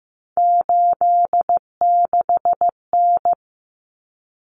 Morse code, heard as a keyed tone, about 15 wpm, 700 Hz. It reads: 86N
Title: Common Cut Numbers - 3-Digit 15wpm